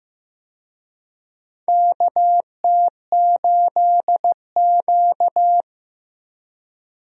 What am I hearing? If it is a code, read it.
Morse code, heard as a keyed tone, about 15 wpm, 700 Hz. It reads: KT8Q
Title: USA Call Signs - 4 Characters - Repeated 2x - 15wpm